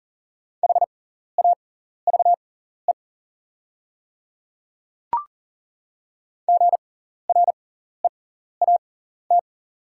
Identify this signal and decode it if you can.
Morse code, heard as a keyed tone, about 40 wpm, spaced as for 14 wpm, 700 Hz. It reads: HAVE GREAT
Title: 6x Character Spacing - Top 200 Words - 40wpm